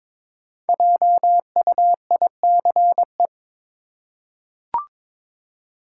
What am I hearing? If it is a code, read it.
Morse code, heard as a keyed tone, about 22 wpm, 690 Hz. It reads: JUICE
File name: 5 Letter Words - First 3 Letters Given - 22wpm